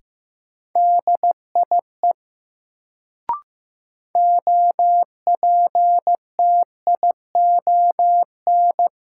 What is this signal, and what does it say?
Morse code, heard as a keyed tone, about 15 wpm, 700 Hz. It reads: DIE OPTION
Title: Sentences from Top 1300 Words - New Word Drill - 15wpm